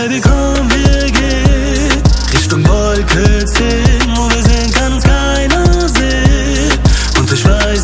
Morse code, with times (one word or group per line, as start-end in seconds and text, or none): none